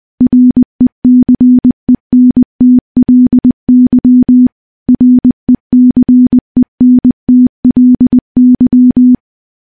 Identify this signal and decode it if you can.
Morse code, heard as a keyed tone, about 21 wpm, 250 Hz. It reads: RECENTLY RECENTLY
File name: morse-training-20WPM